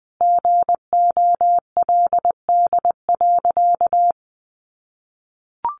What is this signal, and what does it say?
Morse code, heard as a keyed tone, about 19 wpm, 700 Hz. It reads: GOLD.
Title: Sentences from Top 1200 Words 20wpm